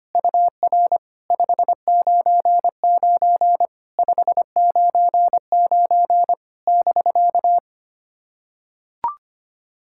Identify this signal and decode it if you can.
Morse code, heard as a keyed tone, about 25 wpm, 700 Hz. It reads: UR 599 599 <BK>
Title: SOTA QSO Practice 25wpm